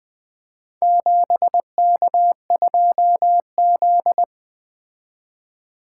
Morse code, heard as a keyed tone, about 20 wpm, 700 Hz. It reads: 7K2Z
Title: Four Letter-Number Combinations - 20wpm